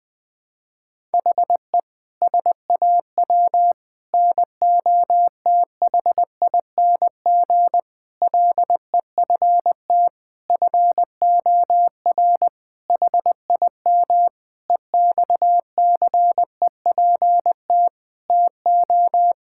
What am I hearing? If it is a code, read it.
Morse code, heard as a keyed tone, about 20 wpm, 700 Hz. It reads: HE SAW NOTHING LEFT FOR HIM EXCEPT TO
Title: Sentences from Top 1600 Words 20wpm